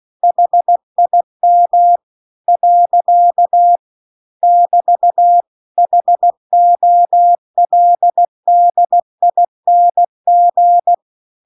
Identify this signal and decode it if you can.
Morse code, heard as a keyed tone, about 16 wpm, 700 Hz. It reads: HIM . <BT> HOLDING
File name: CW-The-Big-Twoheaded-River-2-16WPM